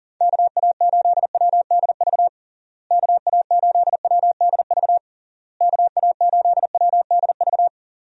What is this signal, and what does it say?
Morse code, heard as a keyed tone, about 40 wpm, 700 Hz. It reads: KA8WDV KA8WDV KA8WDV
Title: USA Call Signs - 6 Characters - Repeated 3x - 40wpm